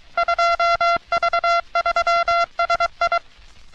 Morse code, heard as a keyed tone, about 23 wpm, 680 Hz. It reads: 2V3SI
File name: U.S. Navy - RN-2849-124 - Sixteens Morse Code Practice